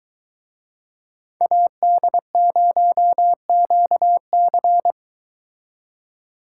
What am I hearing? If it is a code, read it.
Morse code, heard as a keyed tone, about 22 wpm, 700 Hz. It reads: AD0QC